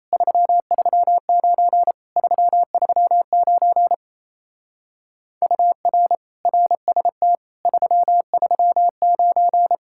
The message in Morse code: 339 339 UR RST 339

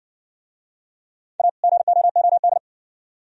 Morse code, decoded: A GOOD

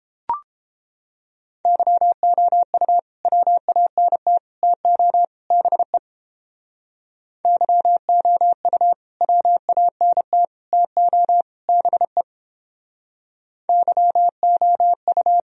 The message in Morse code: YOU WANT TO BE YOU WANT TO BE YOU

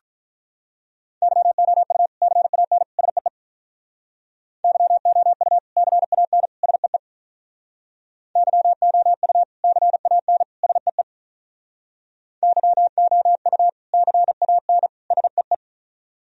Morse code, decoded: YOU CAN SEE YOU CAN SEE YOU CAN SEE YOU CAN SEE

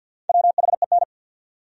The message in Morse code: WHEN

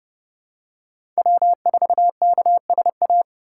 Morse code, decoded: W4KSA

